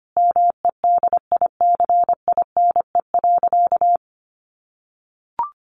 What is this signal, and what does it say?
Morse code, heard as a keyed tone, about 25 wpm, 700 Hz. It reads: MEDICINE.
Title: Sentences from Top 900 Words 25wpm